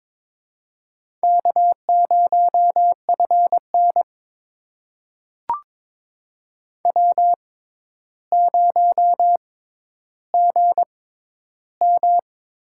Morse code, heard as a keyed tone, about 22 wpm, spaced as for 11 wpm, 700 Hz. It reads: K0FN W0GM
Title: USA Call Signs - 4 Characters - 6x Character Spacing - 22wpm